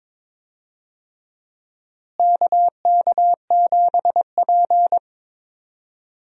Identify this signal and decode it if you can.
Morse code, heard as a keyed tone, about 22 wpm, 700 Hz. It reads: KK7P
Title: USA Call Signs - 4 Characters - Repeated 2x - 22wpm